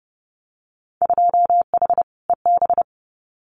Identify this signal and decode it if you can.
Morse code, heard as a keyed tone, about 30 wpm, 700 Hz. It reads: 2H EB
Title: Winter Field Day Exchanges Encore 30wpm